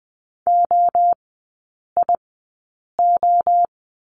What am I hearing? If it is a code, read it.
Morse code, heard as a keyed tone, about 20 wpm, spaced as for 8 wpm, 700 Hz. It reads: OIO
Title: 107 Lesson 18e - Letter m - Three Characters 20wpm